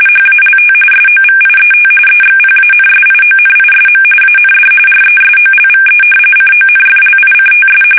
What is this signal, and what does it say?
RTTY, about 45.45 baud. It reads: HALT SOVIET EXPANSIONISM AND RESOLVE CONFLICTS
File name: fsk850 / battleshipnj